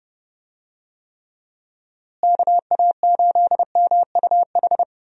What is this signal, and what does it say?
Morse code, heard as a keyed tone, about 32 wpm, 700 Hz. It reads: KA8MUH
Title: USA Call Signs - 6 Characters - Repeated 3x - 30wpm